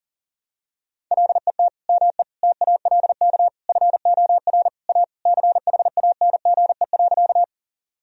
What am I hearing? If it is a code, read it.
Morse code, heard as a keyed tone, about 40 wpm, 700 Hz. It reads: LET ME TALK FOR A CHANGE.
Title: Sentences from Top 1000 Words - Review Part 1 - 40wpm